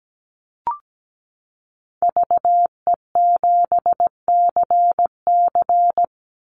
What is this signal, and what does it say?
Morse code, heard as a keyed tone, about 17 wpm, 700 Hz. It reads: VE7CC
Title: Call Signs Repeated 2x - 17wpm